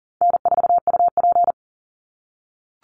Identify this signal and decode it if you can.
Morse code, heard as a keyed tone, about 40 wpm, 700 Hz. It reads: N4UP
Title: Call Signs - 40wpm